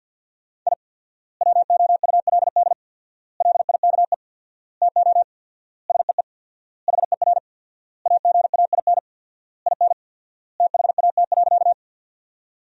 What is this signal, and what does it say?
Morse code, heard as a keyed tone, about 49 wpm, 700 Hz. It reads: I WOULD LIKE TO SEE HER AGAIN IN THAT.
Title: 4x Word-Spacing - Sentences from Top 200 Words - 50wpm